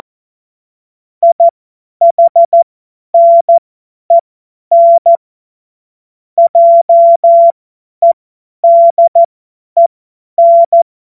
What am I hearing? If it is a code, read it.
Morse code, heard as a keyed tone, about 14 wpm, spaced as for 10 wpm, 680 Hz. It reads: IHNEN JEDEN
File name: Witz-01-Juli-2025-50-F70